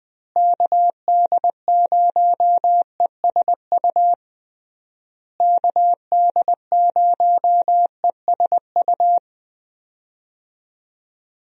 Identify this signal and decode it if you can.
Morse code, heard as a keyed tone, about 21 wpm, 700 Hz. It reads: KD0ESU KD0ESU
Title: USA Call Signs - 6 Characters - Repeated 3x - 20wpm